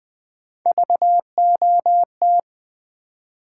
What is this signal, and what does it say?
Morse code, heard as a keyed tone, about 20 wpm, 700 Hz. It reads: VOT
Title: Build A Word - 1 Letter At A Time - 4-Letter Words - 20wpm